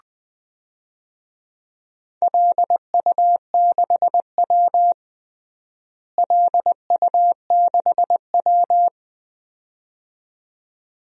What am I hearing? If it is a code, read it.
Morse code, heard as a keyed tone, about 21 wpm, 700 Hz. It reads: LU6W LU6W
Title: DXCC-100 Call Signs - 4 Characters - Repeated 2x - 20wpm